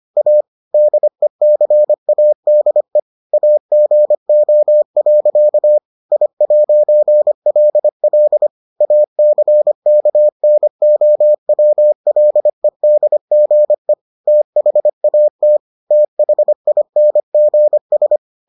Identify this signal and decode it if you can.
Morse code, heard as a keyed tone, about 25 wpm, 600 Hz. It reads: A DECADE AGO. I'LL ACKNOWLEDGE THAT THINGS